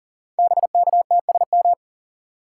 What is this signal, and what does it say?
Morse code, heard as a keyed tone, about 39 wpm, 690 Hz. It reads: BKTSM
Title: Code Groups - 5 Letters - 40wpm